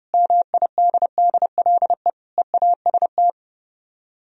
Morse code, heard as a keyed tone, about 40 wpm, 700 Hz. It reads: MIDDLE EAST